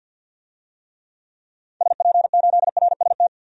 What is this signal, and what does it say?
Morse code, heard as a keyed tone, about 50 wpm, 700 Hz. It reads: SP8RST
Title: DXCC-100 Call Signs - 6 Characters - 50wpm